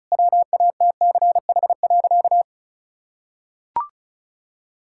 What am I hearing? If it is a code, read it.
Morse code, heard as a keyed tone, about 34 wpm, 700 Hz. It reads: WATCH.
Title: Sentences from Top 1600 Words 35wpm